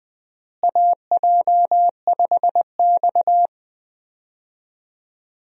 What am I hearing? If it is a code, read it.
Morse code, heard as a keyed tone, about 20 wpm, 700 Hz. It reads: AJ5X
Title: USA Call Signs - 4 Characters - Repeated 2x - 20wpm